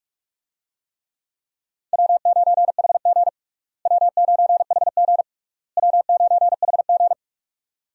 Morse code, heard as a keyed tone, about 41 wpm, 700 Hz. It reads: W9HG W9HG W9HG
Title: USA Call Signs - 4 Characters - Repeated 3x - 45wpm